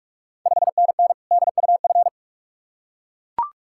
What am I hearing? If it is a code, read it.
Morse code, heard as a keyed tone, about 42 wpm, 700 Hz. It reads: 5NN DUF
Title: Ontario QSO Party Exchange 45wpm